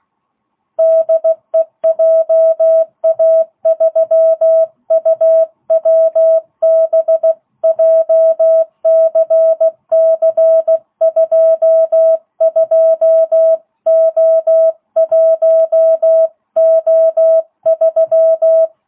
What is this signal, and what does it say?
Morse code, heard as a keyed tone, about 16 wpm, 660 Hz. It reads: DE JA3UWB JCC22O1O3